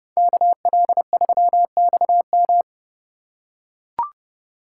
Morse code, heard as a keyed tone, about 30 wpm, 700 Hz. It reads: KL3XM